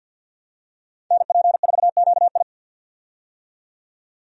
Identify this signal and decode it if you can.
Morse code, heard as a keyed tone, about 52 wpm, 700 Hz. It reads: NP4QI
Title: USA Call Signs - 5 Characters - 50wpm